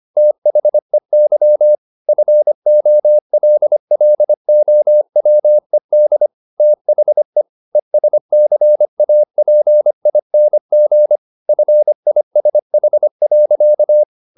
Morse code, heard as a keyed tone, about 25 wpm, 600 Hz. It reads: THEY FOLLOWED THE ESCAPING FISH.
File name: the old man and the sea part 1 25wpm